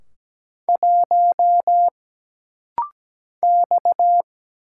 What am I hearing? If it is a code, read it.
Morse code, heard as a keyed tone, about 17 wpm, 700 Hz. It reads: 1 X
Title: Single Letter-Number Alt - Rapid-Fire 17wpm